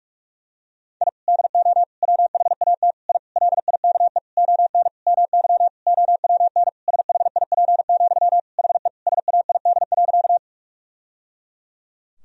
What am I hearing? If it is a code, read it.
Morse code, heard as a keyed tone, about 42 wpm, 700 Hz. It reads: I DO WHAT I LIKE ON MY OWN SHIP, HE SAID.
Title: Sentences from Top 1000 Words - Review Part 2 - 45wpm